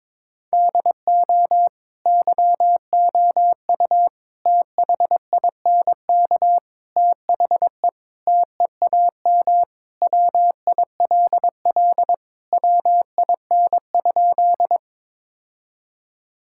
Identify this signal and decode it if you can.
Morse code, heard as a keyed tone, about 22 wpm, 700 Hz. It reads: DO YOU THINK THE TEAM WILL WIN?
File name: practice / 22wpm / Sentences with 4-Word Phrases - 22wpm